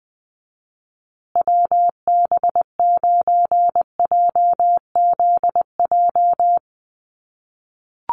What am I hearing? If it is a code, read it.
Morse code, heard as a keyed tone, about 20 wpm, 700 Hz. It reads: WB9JZJ
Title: USA Call Signs Repeated 3x 20wpm